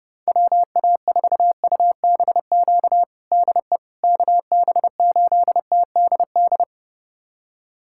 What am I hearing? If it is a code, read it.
Morse code, heard as a keyed tone, about 29 wpm, 700 Hz. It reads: WA4UBQ DE KB8TDD
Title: USA Callsign DE Callsign - 6 Characters - 30wpm